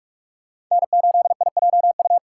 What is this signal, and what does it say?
Morse code, heard as a keyed tone, about 45 wpm, 700 Hz. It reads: N8IJU